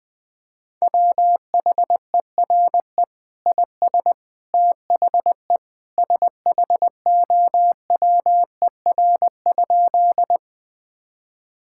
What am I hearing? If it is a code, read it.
Morse code, heard as a keyed tone, about 20 wpm, 700 Hz. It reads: WHERE IS THE SHOWER?
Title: Sentences from Top 1600 Words 20wpm